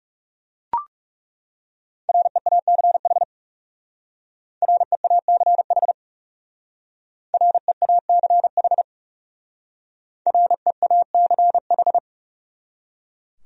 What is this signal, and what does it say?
Morse code, heard as a keyed tone, about 38 wpm, 700 Hz. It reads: REACH REACH REACH REACH